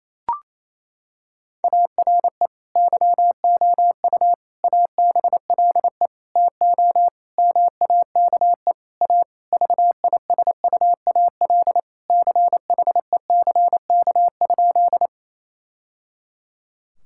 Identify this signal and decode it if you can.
Morse code, heard as a keyed tone, about 28 wpm, 700 Hz. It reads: ARE YOU ABLE TO MAKE A VISUAL CHECK?
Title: Sentences from Top 1000 Words - Review Part 6 - 28wpm